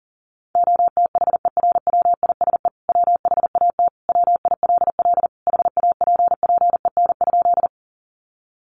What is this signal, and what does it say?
Morse code, heard as a keyed tone, about 40 wpm, 700 Hz. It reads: OTHERWISE WHAT WILL HAPPEN?
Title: Sentences from Top 1200 Words 40wpm